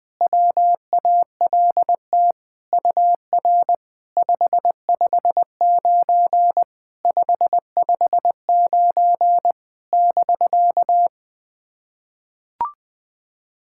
Morse code, heard as a keyed tone, about 20 wpm, 690 Hz. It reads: WALT UR 559 559 <BK>
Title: SOTA QSO Practice 20wpm